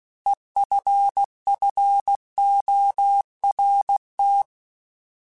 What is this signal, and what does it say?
Morse code, heard as a keyed tone, about 16 wpm, 800 Hz. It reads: EFFORT
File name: Alphabetical / Alphabetical mp3s / Es @15